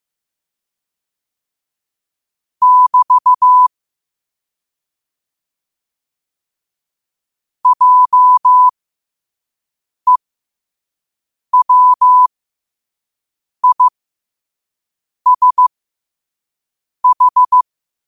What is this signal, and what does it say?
Morse code, heard as a keyed tone, about 15 wpm, spaced as for 5 wpm, 1000 Hz. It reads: <BT> JEWISH
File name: qotdfiles / 5wpm